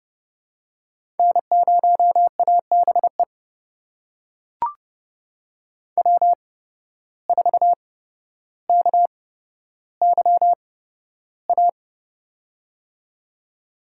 Morse code, heard as a keyed tone, about 30 wpm, spaced as for 13 wpm, 700 Hz. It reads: N0ABE W4KYA
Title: USA Call Signs - 5 Characters - 8x Character Spacing - 30wpm